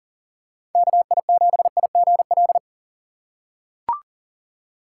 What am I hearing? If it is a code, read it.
Morse code, heard as a keyed tone, about 40 wpm, 700 Hz. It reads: KI7IGL
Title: USA Call Signs - 6 Characters - 3x Character Spacing - 40wpm